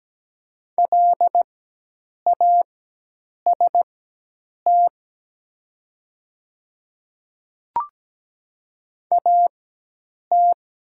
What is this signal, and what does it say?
Morse code, heard as a keyed tone, about 17 wpm, spaced as for 8 wpm, 700 Hz. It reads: LAST AT
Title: 4x Character Spacing - Top 300 Words - 17wpm